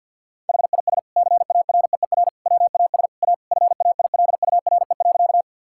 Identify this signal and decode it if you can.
Morse code, heard as a keyed tone, about 50 wpm, 700 Hz. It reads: HIS CAREER WAS A FAILURE.
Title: Sentences from Top 1000 Words - Review Part 6 - 50wpm